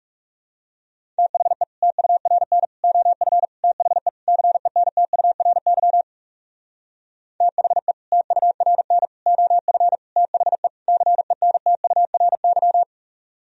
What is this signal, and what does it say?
Morse code, heard as a keyed tone, about 38 wpm, 700 Hz. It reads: THE TURN OF THE CENTURY THE TURN OF THE CENTURY